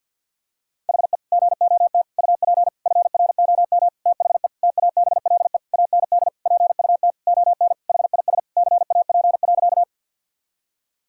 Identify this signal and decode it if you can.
Morse code, heard as a keyed tone, about 50 wpm, 700 Hz. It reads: HE GOT UP FROM THE TABLE AND PUT ON HIS CAP.
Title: Sentences from Top 1600 Words 50wpm